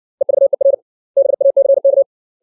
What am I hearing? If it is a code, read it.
Morse code, heard as a keyed tone, about 60 wpm, 550 Hz. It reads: EVER BACK
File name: 632 words / 632 Words 60WPM